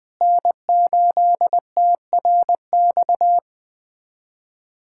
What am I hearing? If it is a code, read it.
Morse code, heard as a keyed tone, about 21 wpm, 700 Hz. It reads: N8TRX
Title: USA Callsign DE Callsign - 5 Characters - 20wpm